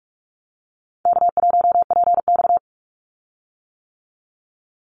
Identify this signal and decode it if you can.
Morse code, heard as a keyed tone, about 44 wpm, 700 Hz. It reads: K1PX
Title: Call Signs - 45wpm